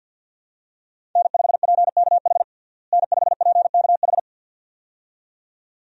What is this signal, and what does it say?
Morse code, heard as a keyed tone, about 47 wpm, 700 Hz. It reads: N5PKH N5PKH
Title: USA Call Signs - 5 Characters - Repeated 2x - 50wpm